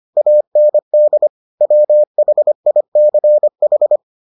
Morse code, heard as a keyed tone, about 25 wpm, 600 Hz. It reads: AND WHICH